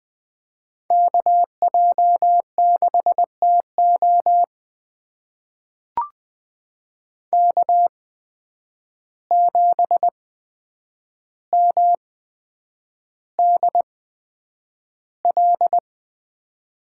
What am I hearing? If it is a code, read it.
Morse code, heard as a keyed tone, about 20 wpm, spaced as for 9 wpm, 700 Hz. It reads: KJ6TO K7MDL
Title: USA Call Signs - 5 Characters - 8x Character Spacing - 20wpm